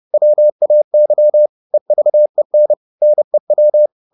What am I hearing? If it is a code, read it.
Morse code, heard as a keyed tone, about 30 wpm, 600 Hz. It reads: WAY EVEN NEW
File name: Top-100-Words-30